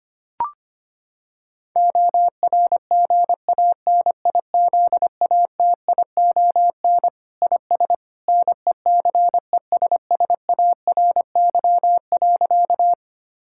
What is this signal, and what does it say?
Morse code, heard as a keyed tone, about 25 wpm, 700 Hz. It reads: ORGANIZATION IS NECESSARY.
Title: Sentences from Top 1000 Words - Review Part 5 - 25wpm